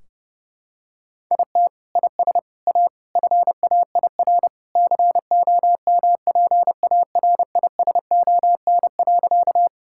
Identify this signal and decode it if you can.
Morse code, heard as a keyed tone, about 30 wpm, 700 Hz. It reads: IT IS A FAIR COMPARISON.
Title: Sentences from Top 1600 Words 30wpm